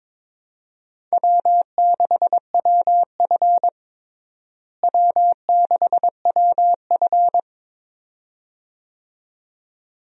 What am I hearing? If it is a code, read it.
Morse code, heard as a keyed tone, about 22 wpm, 700 Hz. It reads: W6WF W6WF
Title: USA Call Signs - 4 Characters - Repeated 2x - 22wpm